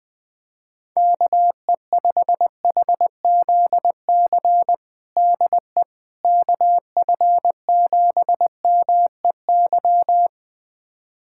Morse code, heard as a keyed tone, about 21 wpm, 700 Hz. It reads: KE5HZC DE KF7MEY
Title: USA Callsign DE Callsign - 6 Characters - 20wpm